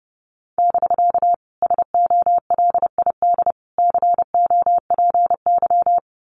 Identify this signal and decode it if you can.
Morse code, heard as a keyed tone, about 32 wpm, 700 Hz. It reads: <BK> SOLID COPY
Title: Common QSO Phrases Encore 30wpm